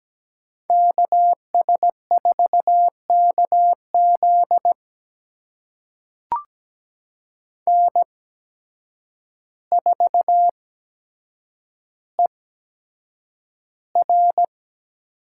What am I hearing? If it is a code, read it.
Morse code, heard as a keyed tone, about 17 wpm, spaced as for 8 wpm, 700 Hz. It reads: KS4KZ N4ER
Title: USA Call Signs - 8x Character Spacing - 17wpm